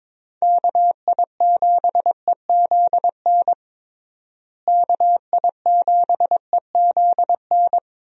Morse code, heard as a keyed tone, about 22 wpm, 700 Hz. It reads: KI7EZN KI7EZN
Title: USA Call Signs - 6 Characters - Repeated 2x - 22wpm